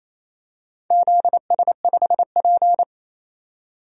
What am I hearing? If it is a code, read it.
Morse code, heard as a keyed tone, about 28 wpm, 700 Hz. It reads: ZS5P
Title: DXCC-100 Call Signs - 4 Characters - 28wpm